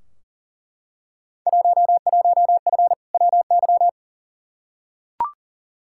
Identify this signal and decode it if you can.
Morse code, heard as a keyed tone, about 40 wpm, 700 Hz. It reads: 11F WY
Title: Field Day Exchanges Encore - 40wpm